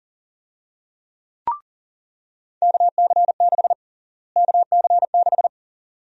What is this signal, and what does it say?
Morse code, heard as a keyed tone, about 40 wpm, 700 Hz. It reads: KC6 KC6
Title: USA Call Sign Prefixes - Repeated 2x - 40wpm